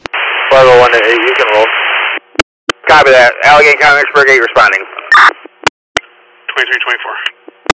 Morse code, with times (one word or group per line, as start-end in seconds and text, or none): none